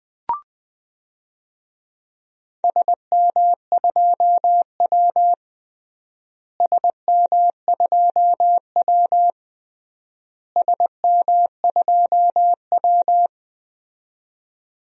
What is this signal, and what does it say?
Morse code, heard as a keyed tone, about 20 wpm, 700 Hz. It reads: SM2W SM2W SM2W
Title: DXCC-100 Call Signs - 4 Characters - Repeated 3x - 20wpm